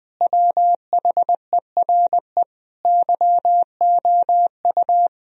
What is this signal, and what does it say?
Morse code, heard as a keyed tone, about 21 wpm, 700 Hz. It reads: WHERE YOU
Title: Sentences from Top 1000 Words - Review Part 4 - 20wpm